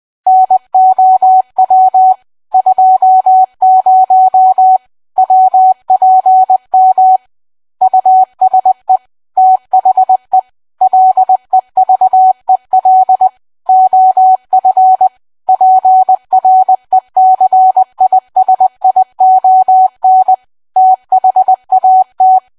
Morse code, heard as a keyed tone, about 20 wpm, 750 Hz. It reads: NOW 20 WPM USE THE LEVEL OF PRECISION THAT